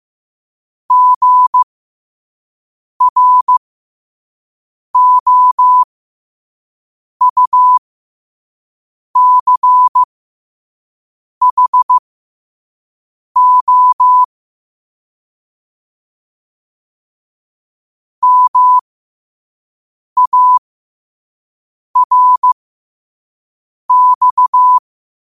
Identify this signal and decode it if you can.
Morse code, heard as a keyed tone, about 15 wpm, spaced as for 5 wpm, 1000 Hz. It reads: GROUCHO MARX
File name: qotdfiles / 5wpm